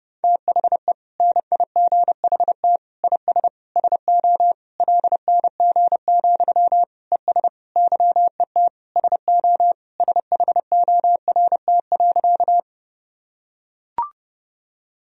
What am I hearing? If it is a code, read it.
Morse code, heard as a keyed tone, about 30 wpm, 700 Hz. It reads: THE NIGHT IS SO LNG, ES YET SO SHORT.
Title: Sentences from Top 500 Words and Abbreviations 30wpm